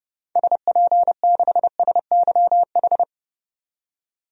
Morse code, heard as a keyed tone, about 31 wpm, 700 Hz. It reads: SP6SYH
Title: DXCC-100 Call Signs - 6 Characters - 30wpm